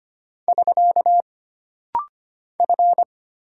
Morse code, heard as a keyed tone, about 25 wpm, 700 Hz. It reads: <SK> F
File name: Single Character (Letter:Number:Punctuation:Prosign) - Rapid-Fire - 25wpm